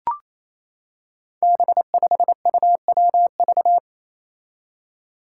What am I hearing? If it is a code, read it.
Morse code, heard as a keyed tone, about 28 wpm, 700 Hz. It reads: B5UWV
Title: Five Challenging Letter-Number Combinations 28wpm